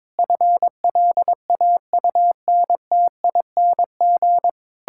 Morse code, heard as a keyed tone, about 22 wpm, 700 Hz. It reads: FLAUNTING
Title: Pangram Challenge v2 - 22wpm